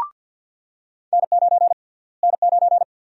Morse code, heard as a keyed tone, about 49 wpm, 700 Hz. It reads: N9 N9
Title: USA Call Sign Prefixes - Repeated 2x - 50wpm